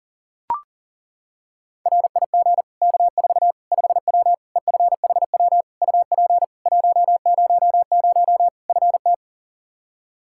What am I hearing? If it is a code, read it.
Morse code, heard as a keyed tone, about 40 wpm, 700 Hz. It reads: RIG K4 5W EFHW UP 100 FT
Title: Practice QSO - Exchange 2 - Info - 40wpm